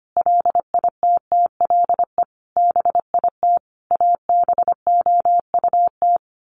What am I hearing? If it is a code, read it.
Morse code, heard as a keyed tone, about 25 wpm, 700 Hz. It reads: LITTLE BIT ABOUT
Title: Sets of 5 Words Encore 25wpm